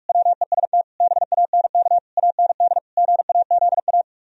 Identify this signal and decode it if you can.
Morse code, heard as a keyed tone, about 44 wpm, 700 Hz. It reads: WEST BANK AND GAZA